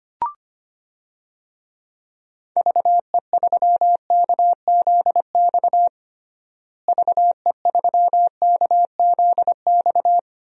Morse code, heard as a keyed tone, about 25 wpm, 700 Hz. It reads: VE3KZX VE3KZX
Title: DXCC-100 Call Signs - 6 Characters - Repeated 3x - 25wpm